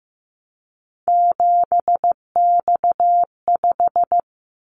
Morse code, heard as a keyed tone, about 15 wpm, 700 Hz. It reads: 7X5
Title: Callsign Prefixes for Top 100 DXCC - 15wpm